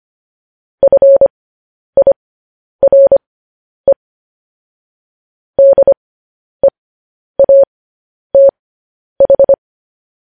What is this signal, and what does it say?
Morse code, heard as a keyed tone, about 25 wpm, spaced as for 10 wpm, 550 Hz. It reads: FIRE DEATH
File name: podcast / News headlines-1x-25x10